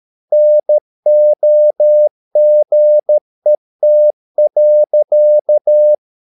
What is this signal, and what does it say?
Morse code, heard as a keyed tone, about 13 wpm, 600 Hz. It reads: NOGET.
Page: Ferie 13 wpm